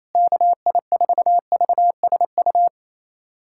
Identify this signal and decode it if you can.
Morse code, heard as a keyed tone, about 28 wpm, 700 Hz. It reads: KI4VSU
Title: USA Callsign DE Callsign - 6 Characters - 28wpm